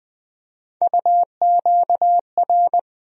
Morse code, heard as a keyed tone, about 20 wpm, 700 Hz. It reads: UQR
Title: USA Call Sign Postfixes - 3 Letters 20wpm